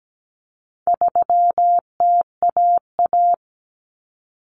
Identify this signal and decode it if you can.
Morse code, heard as a keyed tone, about 17 wpm, 700 Hz. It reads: 3TAA
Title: Common Cut Numbers - 4-Digit 17wpm